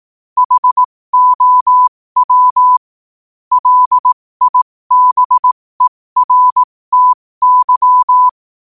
Morse code, heard as a keyed tone, about 18 wpm, spaced as for 15 wpm, 1000 Hz. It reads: HOW LIBERTY
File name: qotdfiles / 15wpm